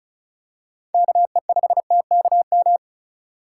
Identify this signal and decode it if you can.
Morse code, heard as a keyed tone, about 36 wpm, 700 Hz. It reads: KE5TKM
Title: USA Call Signs - 6 Characters - 3x Character Spacing - 35wpm